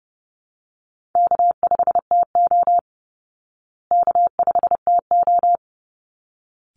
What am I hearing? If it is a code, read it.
Morse code, heard as a keyed tone, about 30 wpm, 700 Hz. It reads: K5TO K5TO
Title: Easy Call Signs Repeated 2x - 30wpm